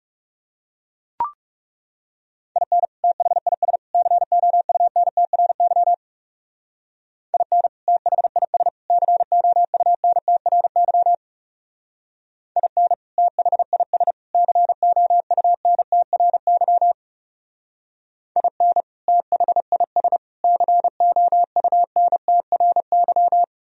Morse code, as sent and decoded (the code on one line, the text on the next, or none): .. -. / - .... .. ... / -.-. --- ..- -. - .-. -.-- / .. -. / - .... .. ... / -.-. --- ..- -. - .-. -.-- / .. -. / - .... .. ... / -.-. --- ..- -. - .-. -.-- / .. -. / - .... .. ... / -.-. --- ..- -. - .-. -.--
IN THIS COUNTRY IN THIS COUNTRY IN THIS COUNTRY IN THIS COUNTRY